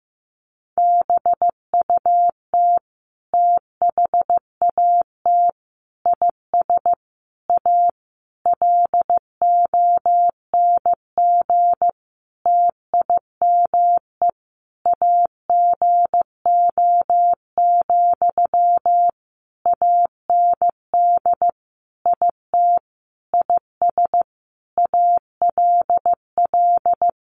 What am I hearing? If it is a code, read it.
Morse code, heard as a keyed tone, about 15 wpm, 700 Hz. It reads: BUT THAT IS A LONG TIME AGO, AND IT IS ALL
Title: Sentences from Top 600 Words 15wpm